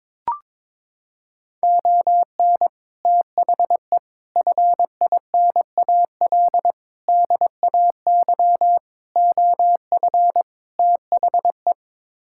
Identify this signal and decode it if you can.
Morse code, heard as a keyed tone, about 22 wpm, 700 Hz. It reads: ON THE FINAL DAY OF THE
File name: practice / 22wpm / Sets of 6 Words - 22wpm